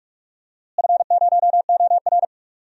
Morse code, heard as a keyed tone, about 46 wpm, 700 Hz. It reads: F0OR
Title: DXCC-100 Call Signs - 4 Characters - Repeated 2x - 45wpm